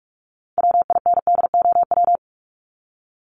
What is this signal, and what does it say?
Morse code, heard as a keyed tone, about 44 wpm, 700 Hz. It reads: WINDOW
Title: Top 1501-1750 Words 45wpm